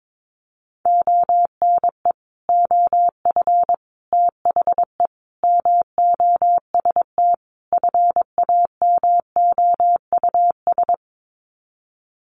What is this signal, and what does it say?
Morse code, heard as a keyed tone, about 22 wpm, 700 Hz. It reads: ONE OF THE MOST FAMOUS
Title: Sets of 5 Words Encore2 22wpm